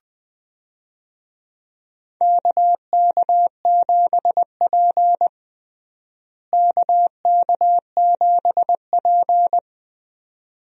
Morse code, heard as a keyed tone, about 20 wpm, 700 Hz. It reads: KK7P KK7P
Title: USA Call Signs - 4 Characters - Repeated 2x - 20wpm